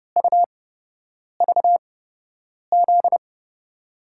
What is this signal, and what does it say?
Morse code, heard as a keyed tone, about 29 wpm, spaced as for 8 wpm, 700 Hz. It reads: UVZ
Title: USA Call Signs - 5 Characters - 8x Character Spacing - 30wpm